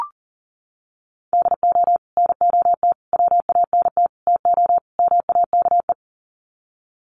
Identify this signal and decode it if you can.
Morse code, heard as a keyed tone, about 40 wpm, 700 Hz. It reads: DO NOT WANT TO MAKE
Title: Sets of 5 Words Encore 40wpm